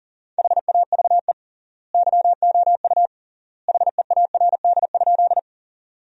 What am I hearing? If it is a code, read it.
Morse code, heard as a keyed tone, about 37 wpm, 700 Hz. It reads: HAVE YOU HEARD?
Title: 3x Word-Spacing - Sentences from Top 400 Words - 40wpm